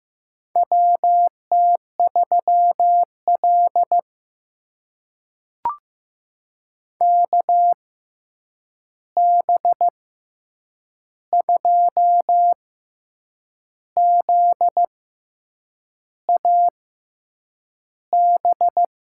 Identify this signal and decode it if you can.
Morse code, heard as a keyed tone, about 15 wpm, spaced as for 7 wpm, 700 Hz. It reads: WT3L KB2ZAB
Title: USA Call Signs - 6x Character Spacing - 15wpm